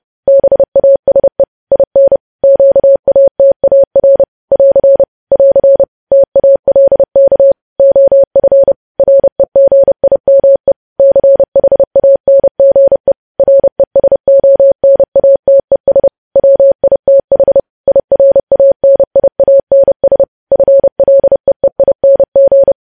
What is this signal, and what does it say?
Morse code, heard as a keyed tone, about 30 wpm, 550 Hz. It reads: BASE IN QATAR <AR> <AR> TALK OF REGIME CHANGE RESONATES WITH IRANIANS FLEEING
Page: news headlines in morse code at 30 wpm